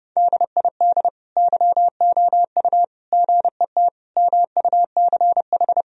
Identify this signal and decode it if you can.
Morse code, heard as a keyed tone, about 32 wpm, 700 Hz. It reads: DID YOU GET MUCH
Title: Sentences from Top 1000 Words - Review Part 3 - 30wpm